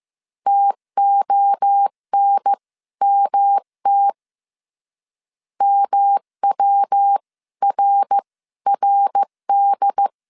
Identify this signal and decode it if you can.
Morse code, heard as a keyed tone, about 15 wpm, spaced as for 13 wpm, 800 Hz. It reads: TONMT MWRRD
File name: CW / DET / 1 3 1500C S A 14wpm 2447